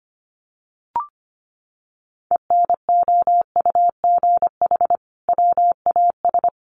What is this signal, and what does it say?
Morse code, heard as a keyed tone, about 25 wpm, 700 Hz. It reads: ENOUGH WAS
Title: Sentences from Top 1400 Words 25wpm